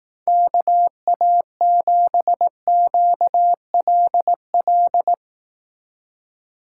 Morse code, heard as a keyed tone, about 18 wpm, 700 Hz. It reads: KA7QLL
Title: Speed Racing - USA 2x3 Call Signs 15wpm